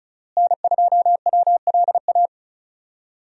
Morse code, heard as a keyed tone, about 36 wpm, 700 Hz. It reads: N2WLA
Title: USA Call Signs - 5 Characters - Repeated 2x - 35wpm